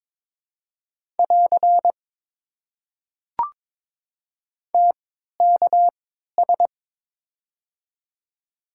Morse code, heard as a keyed tone, about 22 wpm, spaced as for 13 wpm, 690 Hz. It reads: <AR> TKS
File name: Top 90 QSO Elements - 3x Character Spacing - 22wpm